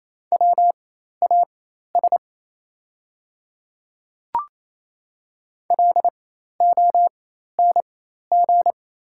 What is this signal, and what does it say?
Morse code, heard as a keyed tone, about 28 wpm, spaced as for 13 wpm, 700 Hz. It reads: WAS LONG
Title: 4x Character Spacing - Top 100 Words - 28wpm